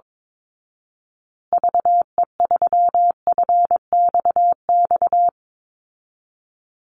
Morse code, heard as a keyed tone, about 22 wpm, 700 Hz. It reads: VE3FXX
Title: Call Signs Repeated 2x - 22wpm